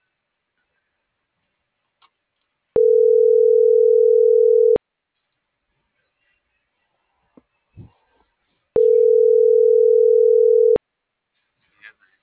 Ringback tone (North America), 440 + 480 Hz.